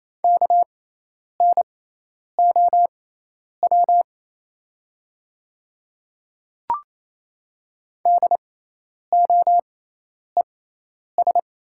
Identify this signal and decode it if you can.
Morse code, heard as a keyed tone, about 28 wpm, spaced as for 10 wpm, 700 Hz. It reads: KNOW DOES
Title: 6x Character Spacing - Top 500 Words - 28wpm